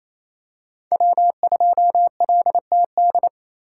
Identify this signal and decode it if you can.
Morse code, heard as a keyed tone, about 28 wpm, 700 Hz. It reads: W2LTD
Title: USA Call Signs - 2x Character Spacing - 28wpm